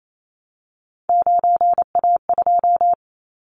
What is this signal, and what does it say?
Morse code, heard as a keyed tone, about 28 wpm, 700 Hz. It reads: 9A2
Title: Callsign Prefixes for Top 100 DXCC - 28wpm